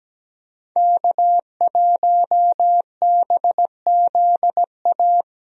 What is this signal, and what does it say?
Morse code, heard as a keyed tone, about 17 wpm, 700 Hz. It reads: K1BZA
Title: USA Call Signs - 5 Characters - Repeated 3x - 17wpm